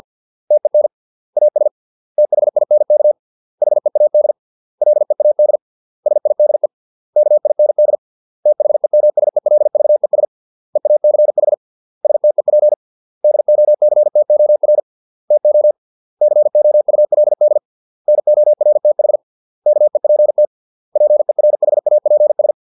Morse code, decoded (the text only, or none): TEN AS THINK HEAD LEAD SIDE KIND THEMSELVES EACH STEP DOCTOR TO COULD NORTH KEPT PERHAPS